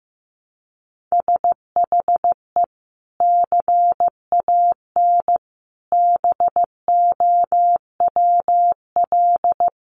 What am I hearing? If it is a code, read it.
Morse code, heard as a keyed tone, about 15 wpm, 700 Hz. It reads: SHE CAN BOWL
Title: Sentences from Top 1500 Words 15wpm